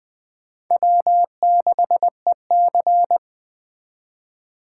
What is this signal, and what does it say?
Morse code, heard as a keyed tone, about 21 wpm, 700 Hz. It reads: W6EC
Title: Speed Racing - USA Call Signs 17wpm